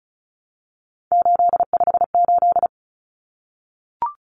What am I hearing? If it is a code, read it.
Morse code, heard as a keyed tone, about 34 wpm, 700 Hz. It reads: 858
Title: Numbers - 3-Digit 35wpm